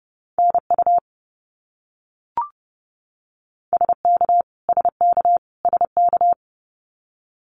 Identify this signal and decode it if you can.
Morse code, heard as a keyed tone, about 30 wpm, 700 Hz. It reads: NU SK SK SK
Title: Canadian Provinces and Territories - Repeated 3x - 30wpm